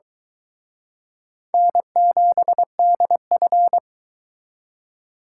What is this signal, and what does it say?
Morse code, heard as a keyed tone, about 23 wpm, 700 Hz. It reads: N7DF